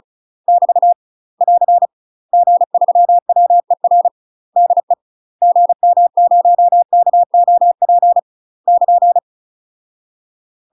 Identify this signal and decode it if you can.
Morse code, heard as a keyed tone, about 36 wpm, 700 Hz. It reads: <BT> <AR> G3WER DE GM0KOP <KN>
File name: UK MASTER QSOs TRAINING / 35WPM / UK MASTER QSOs 0014